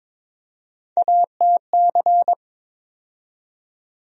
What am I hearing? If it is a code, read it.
Morse code, heard as a keyed tone, about 22 wpm, 700 Hz. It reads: ATC
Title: USA Call Sign Postfixes - 3 Letters 22wpm